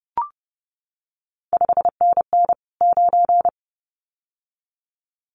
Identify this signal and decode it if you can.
Morse code, heard as a keyed tone, about 30 wpm, 700 Hz. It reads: 5NN 9
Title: CQ WW DX Contest Exchange Encore - 30wpm